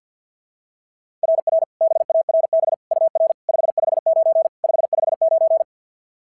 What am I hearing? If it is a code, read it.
Morse code, heard as a keyed tone, about 46 wpm, 660 Hz. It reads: RR BARB UR 559 559
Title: SOTA QSO Practice 50wpm